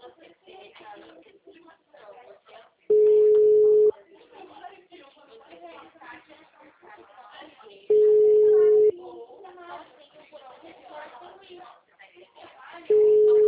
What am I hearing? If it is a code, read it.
Ringback tone (Europe, 425 Hz), with speech.
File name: out- -2012- - - .4